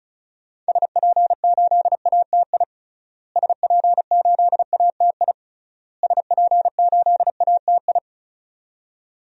Morse code, heard as a keyed tone, about 36 wpm, 700 Hz. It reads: SP8ATI SP8ATI SP8ATI
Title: DXCC-100 Call Signs - 6 Characters - Repeated 3x - 35wpm